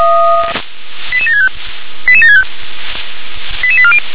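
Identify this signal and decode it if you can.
Five-tone selective call (ZVEI), heard as quick runs of five tones, tones of about 70 ms. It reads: 88765 88765 88840
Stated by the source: ZVEI five-tone selective call received by SDR.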